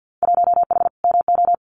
Morse code, heard as a keyed tone, about 50 wpm, 700 Hz. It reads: 1H MO
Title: Winter Field Day Exchanges 50wpm